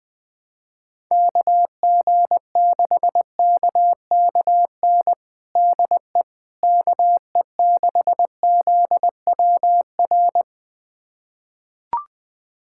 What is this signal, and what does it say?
Morse code, heard as a keyed tone, about 20 wpm, 700 Hz. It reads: KG6KKN DE KE6ZWR
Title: USA Callsign DE Callsign - 6 Characters - 20wpm